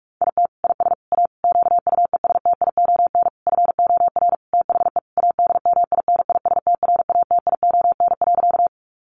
Morse code, heard as a keyed tone, about 44 wpm, 700 Hz. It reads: IT IS A QUESTION FOR THE ADMINISTRATION.
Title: Sentences from Top 1300 Words 45wpm